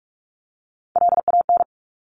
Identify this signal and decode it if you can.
Morse code, heard as a keyed tone, about 46 wpm, 700 Hz. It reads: LAN
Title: Top Three Letter Combinations in Words 45wpm